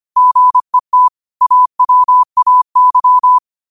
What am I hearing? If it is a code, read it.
Morse code, heard as a keyed tone, about 25 wpm, 1000 Hz. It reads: GET AWAY
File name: qotdfiles / 25wpm